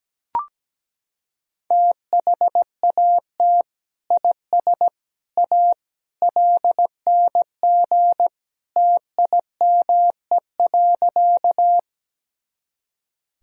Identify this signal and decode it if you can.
Morse code, heard as a keyed tone, about 17 wpm, 700 Hz. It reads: THAT IS A LNG TIME.
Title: Sentences from Top 100 Words and Abbreviations 17wpm